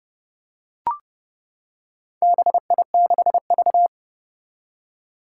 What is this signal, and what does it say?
Morse code, heard as a keyed tone, about 30 wpm, 700 Hz. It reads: BI6V
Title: Four Challenging Letter-Number Combinations 30wpm